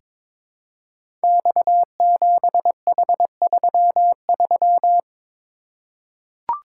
Morse code, heard as a keyed tone, about 21 wpm, 700 Hz. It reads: X7H33
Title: Code Groups - 5 Letter:Numbers - 22wpm